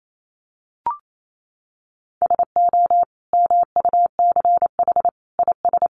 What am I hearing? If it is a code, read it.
Morse code, heard as a keyed tone, about 28 wpm, 700 Hz. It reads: SO MUCH IS
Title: Sentences from Top 1200 Words 28wpm